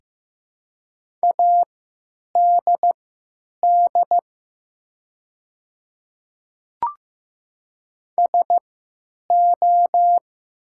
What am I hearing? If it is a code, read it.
Morse code, heard as a keyed tone, about 15 wpm, spaced as for 9 wpm, 700 Hz. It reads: ADD SO
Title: 3x Character Spacing - Top 200 Words - 15wpm